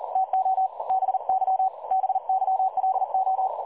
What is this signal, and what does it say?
Morse code, heard as a keyed tone, about 38 wpm, 760 Hz. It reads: TO BY BYLO
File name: dlazaawansowanych / nagraniazpasma / websdr recording 3565.0kHz